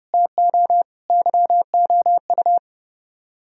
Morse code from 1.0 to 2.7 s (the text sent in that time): YOU